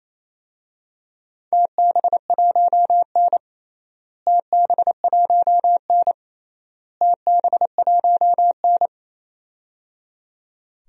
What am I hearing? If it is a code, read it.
Morse code, heard as a keyed tone, about 28 wpm, 700 Hz. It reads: TB1N TB1N TB1N
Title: DXCC-100 Call Signs - 4 Characters - Repeated 3x - 28wpm